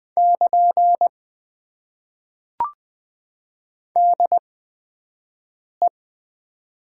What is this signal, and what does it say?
Morse code, heard as a keyed tone, about 20 wpm, spaced as for 5 wpm, 700 Hz. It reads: <KN> DE
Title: Top 90 QSO Elements - 8x Character Spacing - 20wpm